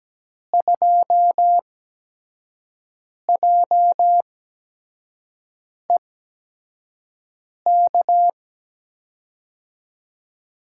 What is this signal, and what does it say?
Morse code, heard as a keyed tone, about 17 wpm, spaced as for 5 wpm, 700 Hz. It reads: 2JEK
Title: USA Call Signs - 5 Characters - 8x Character Spacing - 17wpm